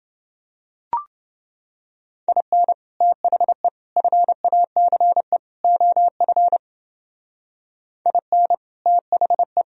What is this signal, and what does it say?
Morse code, heard as a keyed tone, about 29 wpm, 700 Hz. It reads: IN THE FACE OF IN THE